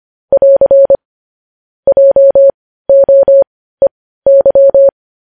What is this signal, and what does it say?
Morse code, heard as a keyed tone, about 25 wpm, spaced as for 15 wpm, 550 Hz. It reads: <AR> JOEY
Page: news-headlines-in-morse-code-at-15-wpm